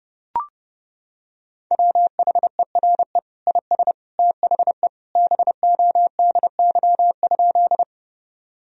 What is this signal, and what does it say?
Morse code, heard as a keyed tone, about 32 wpm, 700 Hz. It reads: WHERE IS THE BODY?
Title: Sentences from Top 1000 Words - Review Part 2 - 30wpm